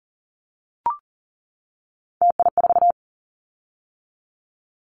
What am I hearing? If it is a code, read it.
Morse code, heard as a keyed tone, about 40 wpm, 700 Hz. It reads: TI4
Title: Callsign Prefixes for Top 100 DXCC - 40wpm